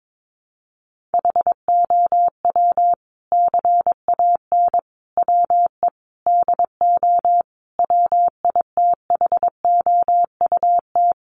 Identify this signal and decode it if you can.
Morse code, heard as a keyed tone, about 22 wpm, 700 Hz. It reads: HOW CAN WE DO WITHOUT
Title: Sentences from Top 600 Words 22wpm